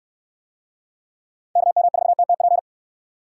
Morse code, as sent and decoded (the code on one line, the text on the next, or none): -.- -- ....- - - .--.
KM4TTP